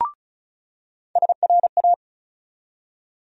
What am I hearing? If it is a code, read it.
Morse code, heard as a keyed tone, about 36 wpm, 700 Hz. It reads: SRA
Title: USA Call Sign Postfixes - 3 Letters 35wpm